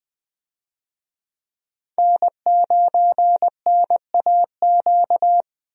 Morse code, heard as a keyed tone, about 20 wpm, 700 Hz. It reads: N9NAQ
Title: USA Call Signs - 5 Characters - Repeated 3x - 20wpm